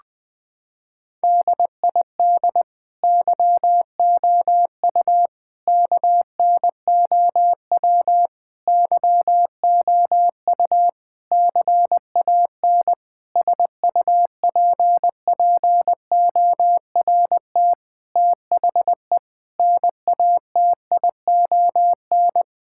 Morse code, heard as a keyed tone, about 20 wpm, 700 Hz. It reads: DID YOU KNOW YOU CAN SUPPORT THE NATION
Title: Sets of 9 Words - 20wpm